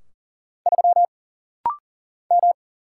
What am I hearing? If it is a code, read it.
Morse code, heard as a keyed tone, about 40 wpm, 700 Hz. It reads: 3 M
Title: Single Letter-Number Alt - Rapid-Fire 40wpm